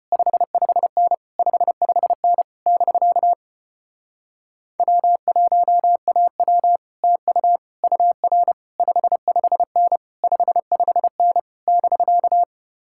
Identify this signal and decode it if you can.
Morse code, heard as a keyed tone, about 33 wpm, 700 Hz. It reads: 55N 55N <BK> W1AW TU UR 55N 55N <BK>